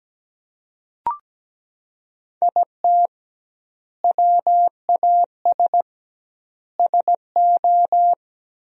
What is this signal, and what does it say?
Morse code, heard as a keyed tone, about 17 wpm, 700 Hz. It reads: IT WAS SO
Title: 2x Word-Spacing - Sentences from Top 500 Words - 17wpm